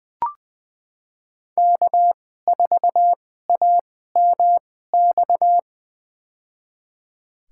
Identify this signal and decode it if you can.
Morse code, heard as a keyed tone, about 20 wpm, spaced as for 14 wpm, 700 Hz. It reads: K4AMX
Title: USA Call Signs - 5 Characters - 2x Character Spacing - 20wpm